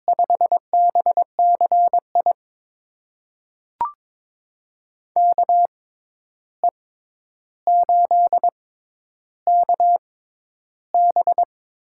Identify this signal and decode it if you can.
Morse code, heard as a keyed tone, about 22 wpm, spaced as for 11 wpm, 700 Hz. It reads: 5BCI KE8KB
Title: USA Call Signs - 6 Characters - 6x Character Spacing - 22wpm